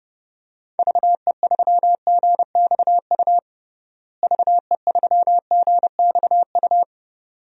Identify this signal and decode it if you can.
Morse code, heard as a keyed tone, about 30 wpm, 700 Hz. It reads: VE3GXU VE3GXU
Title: DXCC-100 Call Signs - 6 Characters - Repeated 2x - 30wpm